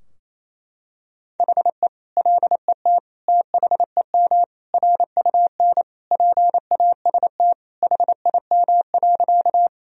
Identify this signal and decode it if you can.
Morse code, heard as a keyed tone, about 28 wpm, 700 Hz. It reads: HE LET THEM RUN PAST HIM.
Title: Sentences from Top 1000 Words - Review Part 3 - 28wpm